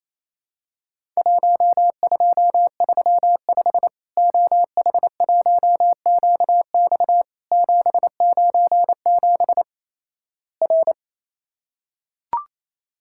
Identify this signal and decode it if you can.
Morse code, heard as a keyed tone, about 28 wpm, 700 Hz. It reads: 1235 OH1QX 797 R
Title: Worked All Europe DX Contest - QTC Exchange - 28wpm